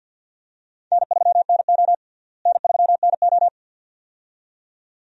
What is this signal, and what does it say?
Morse code, heard as a keyed tone, about 48 wpm, 700 Hz. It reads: N3NO N3NO
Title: USA Call Signs - 4 Characters - Repeated 2x - 50wpm